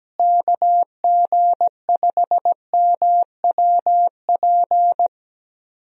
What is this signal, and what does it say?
Morse code, heard as a keyed tone, about 17 wpm, 700 Hz. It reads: KG5MWP
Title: USA Call Signs - 6 Characters - Repeated 3x - 17wpm